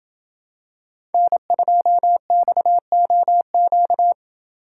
Morse code, heard as a keyed tone, about 28 wpm, 700 Hz. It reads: N2XOQ